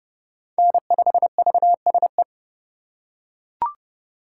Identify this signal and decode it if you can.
Morse code, heard as a keyed tone, about 30 wpm, 700 Hz. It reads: N5VSE
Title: USA Call Signs - 5 Characters - 3x Character Spacing - 30wpm